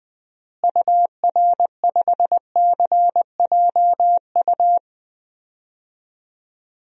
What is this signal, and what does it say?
Morse code, heard as a keyed tone, about 20 wpm, 700 Hz. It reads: UR5CJU
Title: DXCC-100 Call Signs - 6 Characters - Repeated 2x - 20wpm